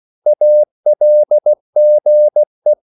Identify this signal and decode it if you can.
Morse code, heard as a keyed tone, about 16 wpm, 600 Hz. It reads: ALGE
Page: Korte danske ord 16wpm